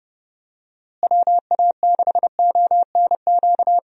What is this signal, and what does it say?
Morse code, heard as a keyed tone, about 30 wpm, 700 Hz. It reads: WA6ONQ
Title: USA Call Signs - 6 Characters - Repeated 2x - 30wpm